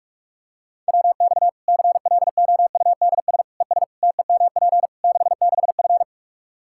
Morse code, heard as a keyed tone, about 43 wpm, 700 Hz. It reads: WX CLOUDS ES TEMP 66F
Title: Practice QSO - Exchange 2 - Info - 45wpm